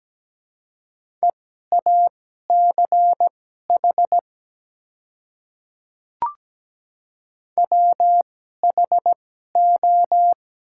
Morse code, heard as a keyed tone, about 17 wpm, spaced as for 12 wpm, 700 Hz. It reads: EACH WHO
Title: 2x Character Spacing - Top 100 Words - 17wpm